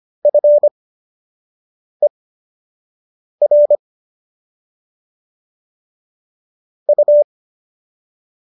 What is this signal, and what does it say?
Morse code, heard as a keyed tone, about 25 wpm, spaced as for 6 wpm, 590 Hz. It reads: FER U